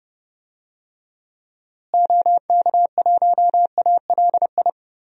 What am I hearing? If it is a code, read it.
Morse code, heard as a keyed tone, about 30 wpm, 700 Hz. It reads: OK1ALI
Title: DXCC-100 Call Signs - 6 Characters - Repeated 3x - 30wpm